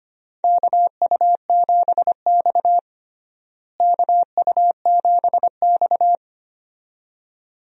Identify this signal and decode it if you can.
Morse code, heard as a keyed tone, about 25 wpm, 700 Hz. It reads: KU7X KU7X
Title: USA Call Signs - 4 Characters - Repeated 2x - 25wpm